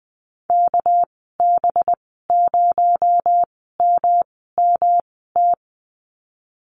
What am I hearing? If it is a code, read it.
Morse code, heard as a keyed tone, about 20 wpm, spaced as for 14 wpm, 700 Hz. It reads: KB0MMT
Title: 231 Lesson 37g - Letter x - Call Signs 20wpm